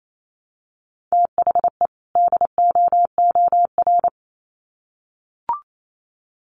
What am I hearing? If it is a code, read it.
Morse code, heard as a keyed tone, about 28 wpm, 700 Hz. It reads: THE DOOR
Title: Sets of 2 Words 28wpm